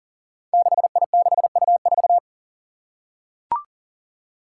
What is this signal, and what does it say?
Morse code, heard as a keyed tone, about 39 wpm, 700 Hz. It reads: 6I6U4
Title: Five Challenging Letter-Number Combinations 40wpm